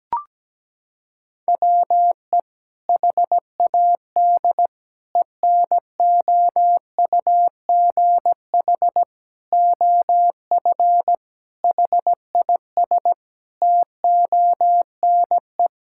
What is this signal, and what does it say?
Morse code, heard as a keyed tone, about 17 wpm, 700 Hz. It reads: WE HAD ENOUGH OF HIS TONE
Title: Sentences from Top 1600 Words 17wpm